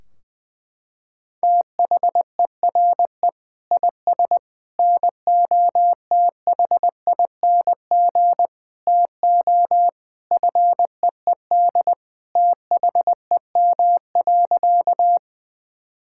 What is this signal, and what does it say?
Morse code, heard as a keyed tone, about 20 wpm, 700 Hz. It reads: THERE IS NOTHING TO FEED THEM.
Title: Sentences from Top 1600 Words 20wpm